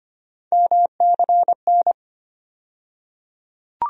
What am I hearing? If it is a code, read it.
Morse code, heard as a keyed tone, about 25 wpm, 700 Hz. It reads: MCN
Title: USA Call Sign Postfixes - 3 Letters 25wpm